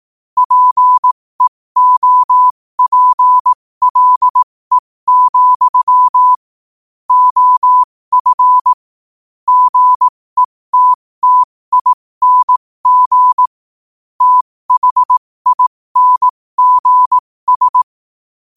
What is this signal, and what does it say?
Morse code, heard as a keyed tone, about 18 wpm, spaced as for 15 wpm, 1000 Hz. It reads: PEOPLE, OF GETTING THINGS
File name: qotdfiles / 15wpm